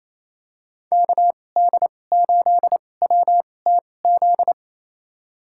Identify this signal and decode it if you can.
Morse code, heard as a keyed tone, about 28 wpm, spaced as for 20 wpm, 700 Hz. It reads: KD8WTZ
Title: USA Call Signs - 2x Character Spacing - 28wpm